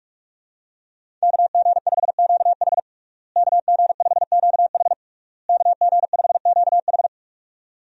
Morse code, heard as a keyed tone, about 46 wpm, 700 Hz. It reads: KG5QH KG5QH KG5QH
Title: USA Call Signs - 5 Characters - Repeated 3x - 45wpm